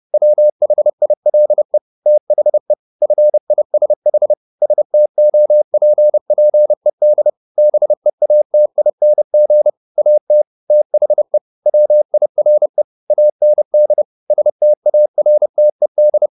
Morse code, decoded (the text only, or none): WHILE THE FISH STOPPED BEATING AT THE WIRE AND STARTED